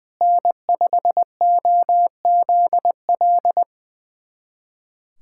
Morse code, heard as a keyed tone, about 21 wpm, 700 Hz. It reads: N5OZL
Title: USA Callsign DE Callsign - 5 Characters - 20wpm